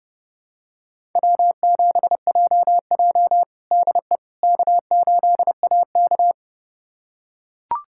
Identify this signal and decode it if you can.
Morse code, heard as a keyed tone, about 30 wpm, 700 Hz. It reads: W7JJ DE K8AK
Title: USA Callsign DE Callsign - 4 Characters - 30wpm